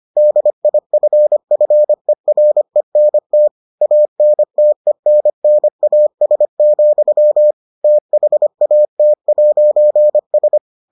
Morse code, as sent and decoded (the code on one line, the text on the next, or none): -.. .. ..-. ..-. .-. -. - / .- -. - . -. -. .- ... --..-- / - .... .- - .----. ...
DIFFERENT ANTENNAS, THAT'S